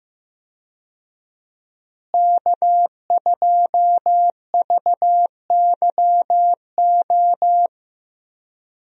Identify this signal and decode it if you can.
Morse code, heard as a keyed tone, about 14 wpm, 700 Hz. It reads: K2VYO
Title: USA Call Signs - 5 Characters - Repeated 3x - 15wpm